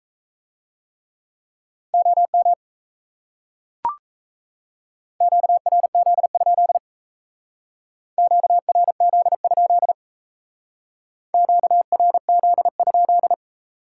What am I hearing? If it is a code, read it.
Morse code, heard as a keyed tone, about 34 wpm, 700 Hz. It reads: OM QRZ? QRZ? QRZ?
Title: Speed-Racing Top 90 QSO Elements 28wpm